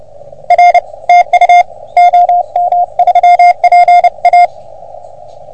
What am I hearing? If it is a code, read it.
Morse code, heard as a keyed tone, about 30 wpm, 720 Hz. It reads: R TU OM3PA
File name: OM3PA PILUP CUC 388